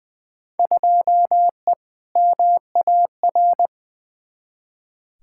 Morse code, heard as a keyed tone, about 20 wpm, 700 Hz. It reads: 2E MAR
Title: Field Day Exchanges Encore - 20wpm